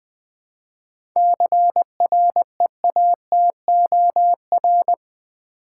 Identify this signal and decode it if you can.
Morse code, heard as a keyed tone, about 20 wpm, 700 Hz. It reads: CREATOR
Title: 7 Letter Words - First 3 Letters Given - 20wpm